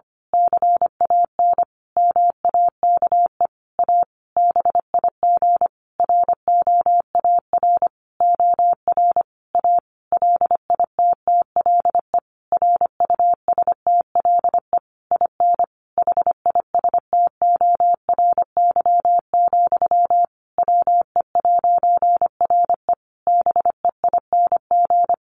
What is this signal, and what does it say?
Morse code, heard as a keyed tone, about 25 wpm, 700 Hz. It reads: CAN MAKE A BIG ROAR OR A LITTLE RUSTLE IN HISTORY, WE'RE BEING
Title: A Sound Of Thunder 25wpm